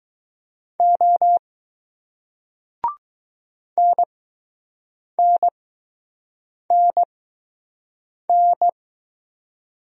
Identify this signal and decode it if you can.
Morse code, heard as a keyed tone, about 20 wpm, spaced as for 6 wpm, 700 Hz. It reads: O NNNN